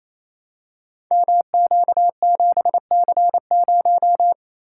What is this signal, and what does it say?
Morse code, heard as a keyed tone, about 29 wpm, 700 Hz. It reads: MQ7C0
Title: Code Groups - 5 Letter:Numbers - 28wpm